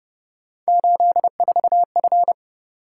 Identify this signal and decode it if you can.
Morse code, heard as a keyed tone, about 32 wpm, 700 Hz. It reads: 84F